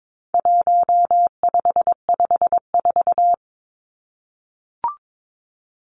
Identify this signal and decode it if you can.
Morse code, heard as a keyed tone, about 22 wpm, 700 Hz. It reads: 1554
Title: Numbers - 4-Digit 22wpm